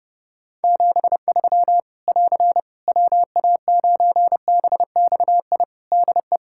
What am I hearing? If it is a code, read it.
Morse code, heard as a keyed tone, about 30 wpm, 700 Hz. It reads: 73 <AR> WA9BXI DE
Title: K1USN SST QSO Practice 30wpm